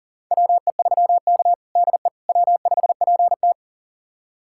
Morse code, heard as a keyed tone, about 40 wpm, 700 Hz. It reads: WE3K DE W5PT
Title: USA Callsign DE Callsign - 4 Characters - 40wpm